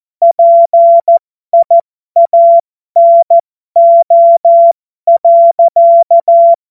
Morse code, heard as a keyed tone, about 14 wpm, spaced as for 12 wpm, 680 Hz. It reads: PIANO.
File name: Morsekurs / Witze